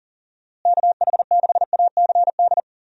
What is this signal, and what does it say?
Morse code, heard as a keyed tone, about 40 wpm, 700 Hz. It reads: KH6ACD
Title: USA Call Signs - 6 Characters - Repeated 3x - 40wpm